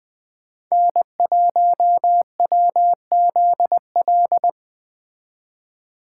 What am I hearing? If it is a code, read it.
Morse code, heard as a keyed tone, about 21 wpm, 700 Hz. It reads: N1WZL
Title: Speed Racing - USA 1x3 Call Signs 17wpm